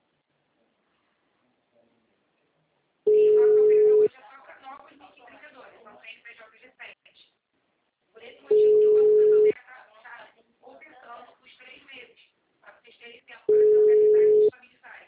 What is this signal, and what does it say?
Ringback tone (Europe, 425 Hz), with speech.
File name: out- -2016- - - .64577